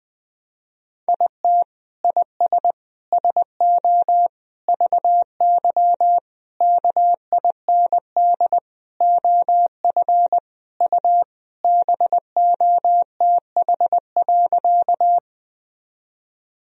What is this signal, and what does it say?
Morse code, heard as a keyed tone, about 20 wpm, 700 Hz. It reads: IT IS SO VY KIND OF U BOTH.
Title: Sentences from Top 300 Words and Abbreviations 20wpm